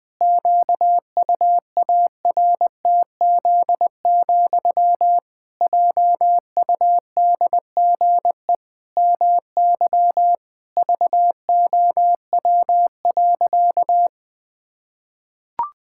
Morse code, heard as a keyed tone, about 20 wpm, 700 Hz. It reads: QUARTZ, JUDGE MY VOW.
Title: Pangram Challenge v2 - 20wpm